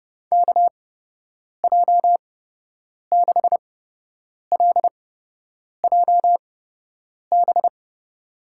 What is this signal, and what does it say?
Morse code, heard as a keyed tone, about 30 wpm, spaced as for 8 wpm, 700 Hz. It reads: KJ6LJB
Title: USA Call Signs - 6 Characters - 8x Character Spacing - 30wpm